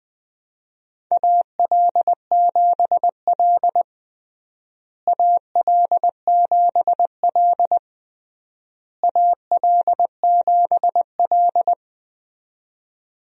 Morse code, heard as a keyed tone, about 21 wpm, 700 Hz. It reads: AL7L AL7L AL7L
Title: USA Call Signs - 4 Characters - Repeated 3x - 20wpm